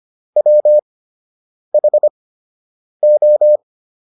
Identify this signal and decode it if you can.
Morse code, heard as a keyed tone, about 25 wpm, spaced as for 8 wpm, 600 Hz. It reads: WHO